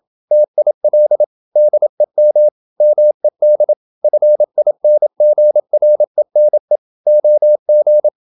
Morse code, heard as a keyed tone, about 27 wpm, 600 Hz. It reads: TIL DEM MED FINGRENE OG
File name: Ferie QRQ 27wpm